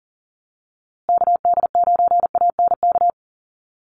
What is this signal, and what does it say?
Morse code, heard as a keyed tone, about 40 wpm, 700 Hz. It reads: KD9ANK
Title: USA Call Signs Repeated 3x 40wpm